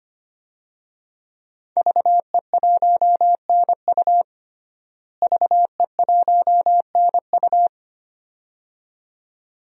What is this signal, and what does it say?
Morse code, heard as a keyed tone, about 25 wpm, 700 Hz. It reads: VE1NU VE1NU
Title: DXCC-100 Call Signs - 5 Characters - Repeated 2x - 25wpm